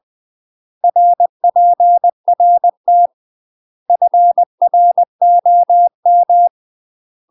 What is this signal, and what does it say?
Morse code, heard as a keyed tone, about 20 wpm, 700 Hz. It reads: RPRT FROM
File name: UK MASTER QSOs TRAINING/20WPM